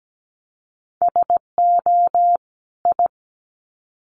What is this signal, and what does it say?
Morse code, heard as a keyed tone, about 17 wpm, 700 Hz. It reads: SO I
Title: Sets of 2 Words 17wpm